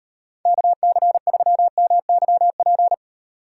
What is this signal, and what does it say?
Morse code, heard as a keyed tone, about 32 wpm, 700 Hz. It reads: KC3MYP